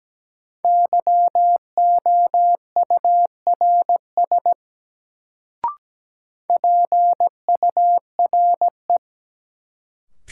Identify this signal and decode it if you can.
Morse code, heard as a keyed tone, about 17 wpm, 700 Hz. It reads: YOURS PURE
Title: Sentences from Top 1500 Words - New Word Drill - 17wpm